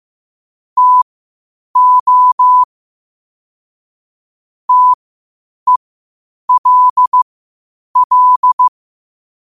Morse code, heard as a keyed tone, about 15 wpm, spaced as for 8 wpm, 1000 Hz. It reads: TO TELL
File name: qotdfiles / 8wpm